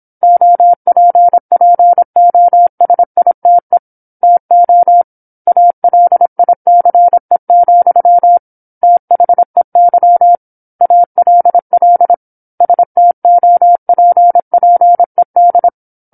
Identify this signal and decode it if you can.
Morse code, heard as a keyed tone, about 27 wpm, 700 Hz. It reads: OPPOSITE TO ALICE, THEY ALL STOPPED
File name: 08-Chapter-08 26 WPM